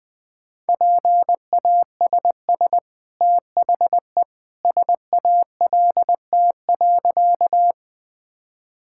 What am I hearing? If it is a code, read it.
Morse code, heard as a keyed tone, about 20 wpm, 700 Hz. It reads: PASS THE SALT.
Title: Easy Sentences with Hints - 20wpm